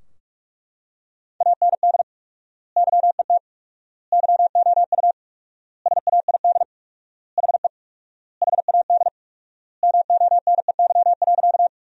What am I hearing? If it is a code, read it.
Morse code, heard as a keyed tone, about 41 wpm, 700 Hz. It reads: AND YET YOU SAID HE HAD MONEY.
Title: 4x Word-Spacing - Sentences from Top 500 Words - 45wpm